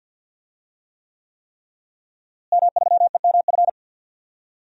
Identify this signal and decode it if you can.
Morse code, heard as a keyed tone, about 47 wpm, 700 Hz. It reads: M3EMF
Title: DXCC-100 Call Signs - 5 Characters - Repeated 3x - 50wpm